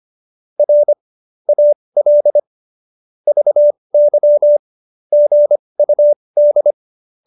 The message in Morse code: R AL VY GUD

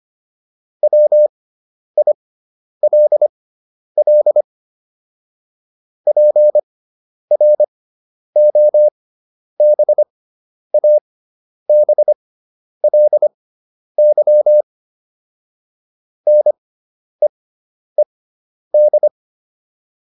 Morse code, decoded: WILL PROBABLY NEED

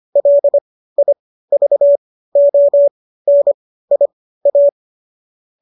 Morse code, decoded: LIVONIA

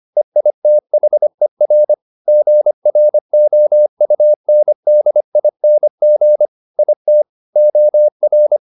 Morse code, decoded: EITHER GROUNDING IT OR

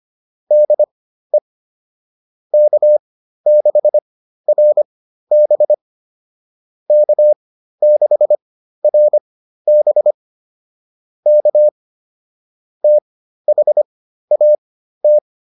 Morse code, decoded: DE K6RB K6RB K THAT

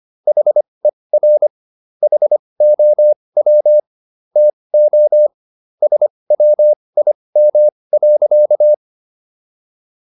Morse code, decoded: HER HOW TO SWIM.

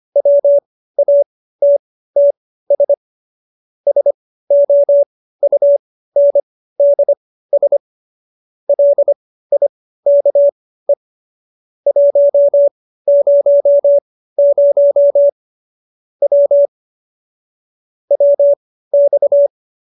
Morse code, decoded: WATTS SOUNDS LIKE 100 W WX

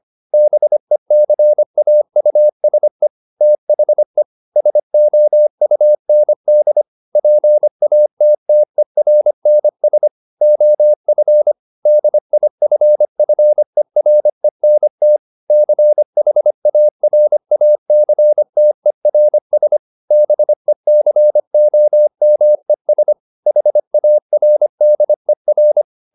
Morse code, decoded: BECAUSE THE SOUND PATTERNS OF DIFFERENT CHARACTERS BECOMES HARDER

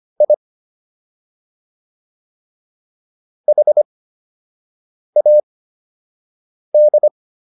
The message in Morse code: I HAD

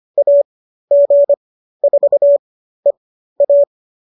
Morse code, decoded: AG4EA

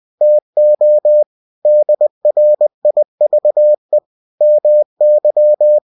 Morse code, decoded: TO DRIVE MY

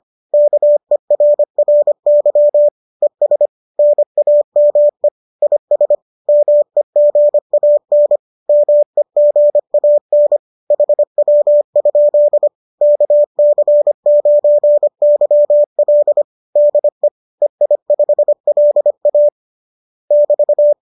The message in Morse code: KERRY ES NAME IS MEGAN MEGAN HW? KC9YL DE EI5LA <BT>

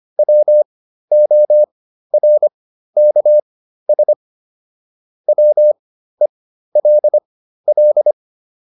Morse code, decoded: WORKS WELL

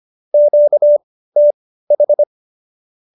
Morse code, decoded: QTH

